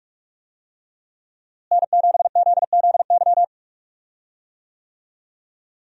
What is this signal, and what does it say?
Morse code, heard as a keyed tone, about 44 wpm, 700 Hz. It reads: N7ZZY
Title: USA Call Signs - 5 Characters - Repeated 3x - 45wpm